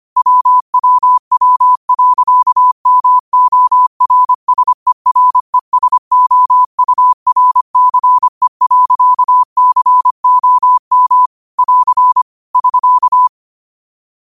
Morse code, heard as a keyed tone, about 25 wpm, 1000 Hz. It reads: WWW.MORSERESOURCE.COM <AR> <SK>